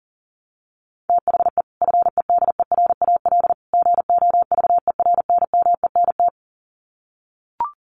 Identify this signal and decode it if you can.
Morse code, heard as a keyed tone, about 40 wpm, 700 Hz. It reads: THE FEDERAL GOVERNMENT
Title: Sets of 3 Words 40wpm